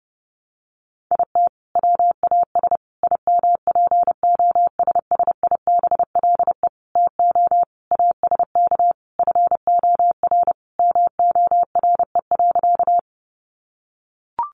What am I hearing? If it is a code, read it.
Morse code, heard as a keyed tone, about 30 wpm, 700 Hz. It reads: IT WAS IMPOSSIBLE TO ASK FOR MORE.
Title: Sentences from Top 1100 Words 30wpm